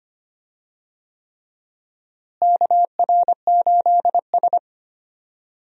Morse code, heard as a keyed tone, about 25 wpm, 700 Hz. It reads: KR8S
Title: USA Call Signs - 4 Characters - Repeated 3x - 25wpm